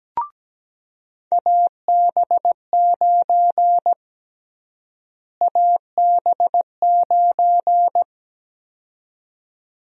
Morse code, heard as a keyed tone, about 17 wpm, 700 Hz. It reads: AB9 AB9
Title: USA Call Sign Prefixes - Repeated 2x - 17wpm